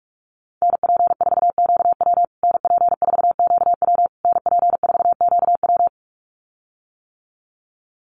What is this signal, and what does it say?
Morse code, heard as a keyed tone, about 44 wpm, 700 Hz. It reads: NP4QW NP4QW NP4QW
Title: USA Call Signs Repeated 3x 45wpm